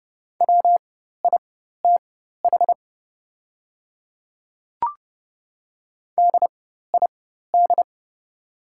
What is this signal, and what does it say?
Morse code, heard as a keyed tone, about 31 wpm, spaced as for 14 wpm, 700 Hz. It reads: WITH DID
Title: 4x Character Spacing - Top 100 Words - 30wpm